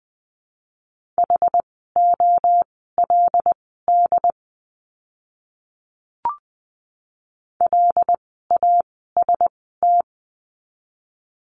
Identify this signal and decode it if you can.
Morse code, heard as a keyed tone, about 20 wpm, spaced as for 15 wpm, 700 Hz. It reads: HOLD LAST
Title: 115 Lesson 19f - Letter w - Words 20wpm